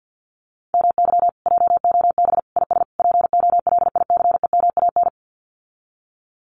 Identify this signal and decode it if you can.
Morse code, heard as a keyed tone, about 48 wpm, 700 Hz. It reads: MY JOB IS POLICEMAN
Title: Common QSO Phrases Encore 50wpm